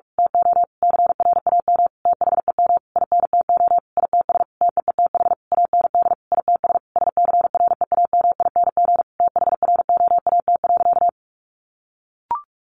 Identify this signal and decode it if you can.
Morse code, heard as a keyed tone, about 44 wpm, 700 Hz. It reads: TO CRAM THEM INTO ITS TEETH AND ITS SCREAMING THROAT.
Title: A Sound Of Thunder 45wpm